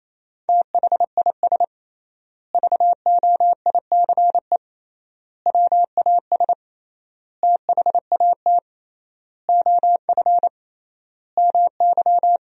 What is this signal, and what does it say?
Morse code, heard as a keyed tone, about 28 wpm, 700 Hz. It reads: THIS VOICE WAS THAT OF MY
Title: 3x Word-Spacing - Sentences from Top 500 Words - 28wpm